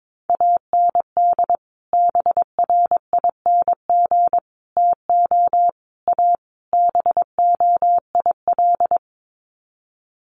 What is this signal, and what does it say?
Morse code, heard as a keyed tone, about 22 wpm, 700 Hz. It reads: AND BRING TO A BOIL
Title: Sets of 5 Words Encore 22wpm